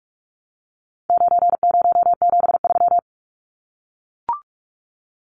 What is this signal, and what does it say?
Morse code, heard as a keyed tone, about 44 wpm, 700 Hz. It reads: 9073
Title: Numbers - 4-Digit 45wpm